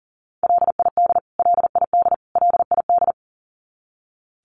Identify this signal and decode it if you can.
Morse code, heard as a keyed tone, about 40 wpm, 700 Hz. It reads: LID LID LID
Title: Abbreviations Repeated 3x - 40wpm